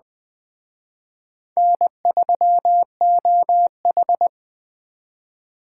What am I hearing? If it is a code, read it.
Morse code, heard as a keyed tone, about 20 wpm, 700 Hz. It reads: N3OH
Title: Speed Racing - USA 1x2 Call Signs 17wpm